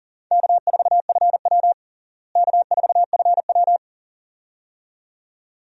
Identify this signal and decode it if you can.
Morse code, heard as a keyed tone, about 37 wpm, 700 Hz. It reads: K4FW K4FW
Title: USA Call Signs - 4 Characters - Repeated 2x - 40wpm